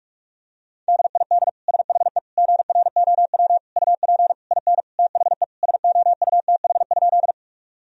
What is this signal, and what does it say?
Morse code, heard as a keyed tone, about 46 wpm, 700 Hz. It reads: DID SHE GROW UP IN THE SOUTH?
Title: Sentences from Top 1000 Words - Review Part 2 - 45wpm